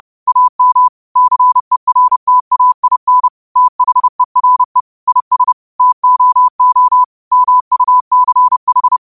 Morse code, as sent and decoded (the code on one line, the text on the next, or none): .- -- / -.-. .-. - .- .. -. / - .... .-. / .. ... / - --- --- / -- ..- -.-. ....
AM CERTAIN THERE IS TOO MUCH